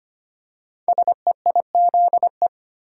Morse code, heard as a keyed tone, about 25 wpm, 700 Hz. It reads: SEIZE
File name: 5 Letter Words - First 3 Letters Given - 25wpm